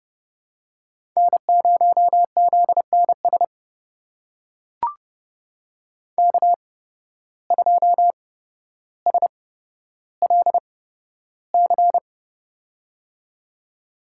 Morse code, heard as a keyed tone, about 31 wpm, spaced as for 13 wpm, 700 Hz. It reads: N0ZNS K2SLC
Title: USA Call Signs - 5 Characters - 8x Character Spacing - 30wpm